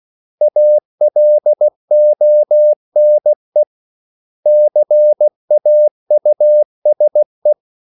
Morse code, heard as a keyed tone, about 16 wpm, 600 Hz. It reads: ALONE CAUSE
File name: CW / Practice / 500MostCommonEnglishWords16